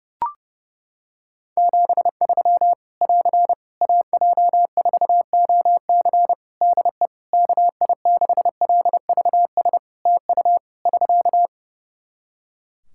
Morse code, heard as a keyed tone, about 30 wpm, 700 Hz. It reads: 73 <AR> AJ4OC DE KI6LVS TU <SK>
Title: Practice QSO - Exchange 4 - Ending - 30wpm